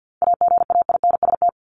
Morse code, heard as a keyed tone, about 50 wpm, 700 Hz. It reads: AGAINST
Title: Top 500 Words 50wpm